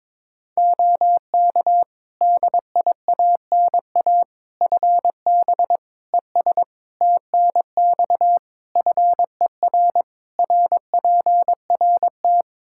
Morse code, heard as a keyed tone, about 22 wpm, 700 Hz. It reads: OK DIANA FB ES TNX FER RPRT
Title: Practice QSO - Exchange 2 - Info - 22wpm